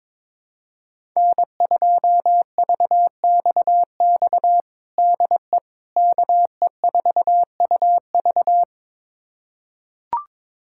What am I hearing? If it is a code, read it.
Morse code, heard as a keyed tone, about 22 wpm, 700 Hz. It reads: N2VXX DE KE4UV
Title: USA Callsign DE Callsign - 5 Characters - 22wpm